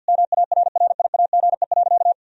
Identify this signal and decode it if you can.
Morse code, heard as a keyed tone, about 51 wpm, 700 Hz. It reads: MARRIAGE.
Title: Sentences from Top 1000 Words - Review Part 1 - 50wpm